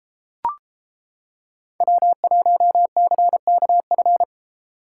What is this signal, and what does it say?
Morse code, heard as a keyed tone, about 30 wpm, 700 Hz. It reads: W1CKF